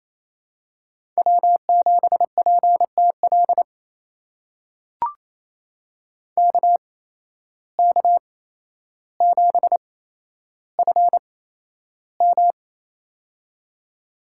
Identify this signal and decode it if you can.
Morse code, heard as a keyed tone, about 28 wpm, spaced as for 12 wpm, 700 Hz. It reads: W7PTL KK7FM
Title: USA Call Signs - 5 Characters - 8x Character Spacing - 28wpm